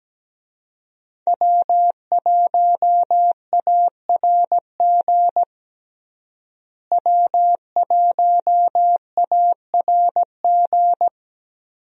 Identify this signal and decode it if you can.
Morse code, heard as a keyed tone, about 17 wpm, 700 Hz. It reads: W1ARG W1ARG
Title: USA Call Signs - 5 Characters - Repeated 2x - 17wpm